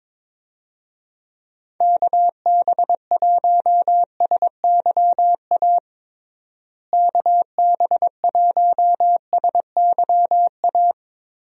Morse code, heard as a keyed tone, about 22 wpm, 700 Hz. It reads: KB1SYA KB1SYA
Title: USA Call Signs - 6 Characters - Repeated 2x - 22wpm